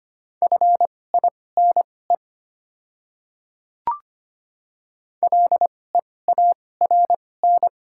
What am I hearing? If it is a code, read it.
Morse code, heard as a keyed tone, about 25 wpm, spaced as for 18 wpm, 690 Hz. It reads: FINE LEARN
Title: 2x Character Spacing - Top 500 Words - 25wpm